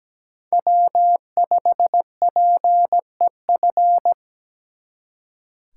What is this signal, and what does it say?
Morse code, heard as a keyed tone, about 20 wpm, 700 Hz. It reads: W5PEF
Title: Speed Racing - USA Call Signs 17wpm